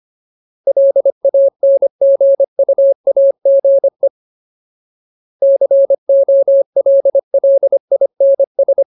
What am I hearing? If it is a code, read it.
Morse code, heard as a keyed tone, about 25 wpm, 550 Hz. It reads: LANGUAGE COLLINS
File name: Ultimate Word List / 25WPM Ultimate